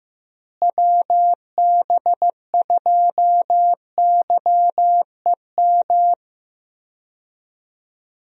Morse code, heard as a keyed tone, about 15 wpm, 700 Hz. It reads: WB2YEM
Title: USA Call Signs - 6 Characters - Repeated 3x - 15wpm